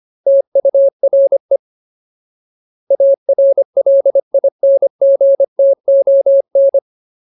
Morse code, heard as a keyed tone, about 25 wpm, 550 Hz. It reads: TURE ARLINGTON
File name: Ultimate Word List / 25WPM Ultimate